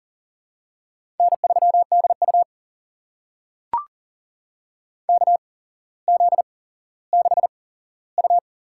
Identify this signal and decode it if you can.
Morse code, heard as a keyed tone, about 40 wpm, spaced as for 17 wpm, 700 Hz. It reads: N3DU KZ6U
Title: USA Call Signs - 4 Characters - 8x Character Spacing - 40wpm